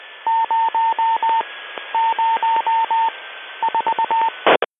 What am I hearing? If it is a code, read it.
Morse code, heard as a keyed tone, about 20 wpm, 910 Hz, with radio noise. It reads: 0 0 4
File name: HUMSAT-D 1158UTC z AF cropped